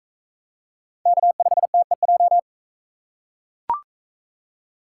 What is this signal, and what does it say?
Morse code, heard as a keyed tone, about 40 wpm, 700 Hz. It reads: K5TEJ